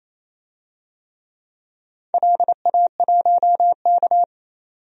Morse code, heard as a keyed tone, about 27 wpm, 700 Hz. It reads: LA1K
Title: DXCC-100 Call Signs - 4 Characters - 28wpm